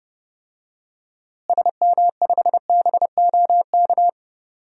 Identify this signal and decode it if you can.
Morse code, heard as a keyed tone, about 30 wpm, 700 Hz. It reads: SM5BOK
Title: DXCC-100 Call Signs - 6 Characters - Repeated 3x - 30wpm